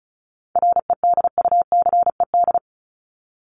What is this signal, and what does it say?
Morse code, heard as a keyed tone, about 34 wpm, 700 Hz. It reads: REDUCED
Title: ED Suffix - Top 100 Words with Base Words 35wpm